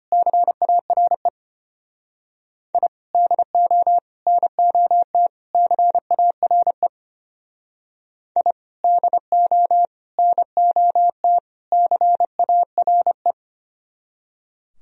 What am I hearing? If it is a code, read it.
Morse code, heard as a keyed tone, about 32 wpm, 700 Hz. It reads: CARE I DO NOT CARE I DO NOT CARE